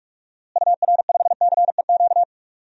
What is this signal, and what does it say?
Morse code, heard as a keyed tone, about 41 wpm, 700 Hz. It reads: UR5CEQ
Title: DXCC-100 Call Signs - 6 Characters - Repeated 2x - 45wpm